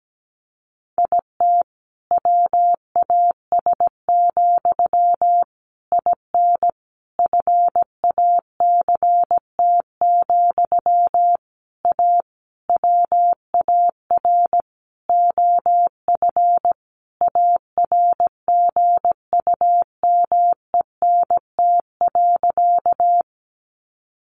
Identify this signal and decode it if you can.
Morse code, heard as a keyed tone, about 17 wpm, 700 Hz. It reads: IT WAS, IN FACT, A WAR OF ARGUMENT.
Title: Sentences from Top 1400 Words 17wpm